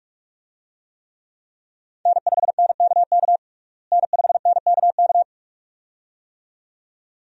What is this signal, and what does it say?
Morse code, heard as a keyed tone, about 42 wpm, 700 Hz. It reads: N5NKK N5NKK
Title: USA Call Signs - 5 Characters - Repeated 2x - 45wpm